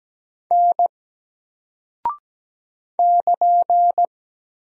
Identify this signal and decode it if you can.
Morse code, heard as a keyed tone, about 17 wpm, 700 Hz. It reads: N <KN>
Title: Single Character (Letter:Number:Punctuation:Prosign) 17wpm